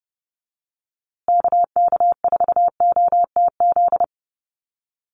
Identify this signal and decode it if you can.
Morse code, heard as a keyed tone, about 30 wpm, 700 Hz. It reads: KK4OTZ
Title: USA Call Signs Repeated 3x 30wpm